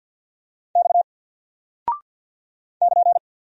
Morse code, heard as a keyed tone, about 48 wpm, 700 Hz. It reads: X <KN>
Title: Single Character (Letter:Number:Punctuation:Prosign) Alt 50wpm